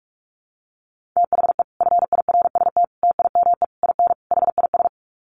Morse code, heard as a keyed tone, about 44 wpm, 700 Hz. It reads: THE FIRST TIME IN HIS
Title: Sets of 5 Words Encore2 45wpm